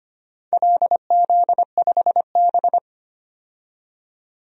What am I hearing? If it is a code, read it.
Morse code, heard as a keyed tone, about 25 wpm, 700 Hz. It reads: LZ5B
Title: DXCC-100 Call Signs - 4 Characters - 25wpm